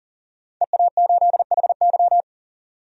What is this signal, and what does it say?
Morse code, heard as a keyed tone, about 40 wpm, 700 Hz. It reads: EA8HY
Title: DXCC-100 Call Signs - 5 Characters - 40wpm